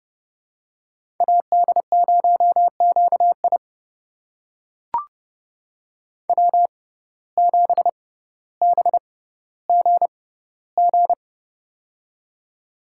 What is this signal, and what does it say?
Morse code, heard as a keyed tone, about 30 wpm, spaced as for 15 wpm, 700 Hz. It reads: AD0QI W7BGG
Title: USA Call Signs - 5 Characters - 6x Character Spacing - 30wpm